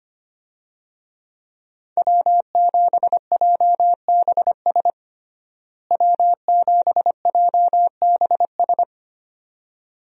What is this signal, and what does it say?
Morse code, heard as a keyed tone, about 25 wpm, 700 Hz. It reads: W7JBS W7JBS
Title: USA Call Signs - 5 Characters - Repeated 2x - 25wpm